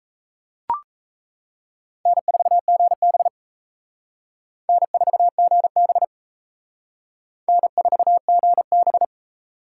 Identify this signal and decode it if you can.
Morse code, heard as a keyed tone, about 34 wpm, 700 Hz. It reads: N4GB N4GB N4GB
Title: Speed Racing - USA 1x2 Call Signs 28wpm